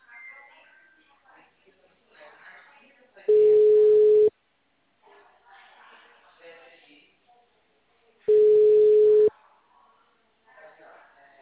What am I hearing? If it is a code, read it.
Ringback tone (Europe, 425 Hz), with speech.